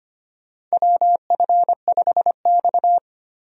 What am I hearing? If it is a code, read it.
Morse code, heard as a keyed tone, about 25 wpm, 700 Hz. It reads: WF5X
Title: USA Call Signs - 4 Characters - Repeated 3x - 25wpm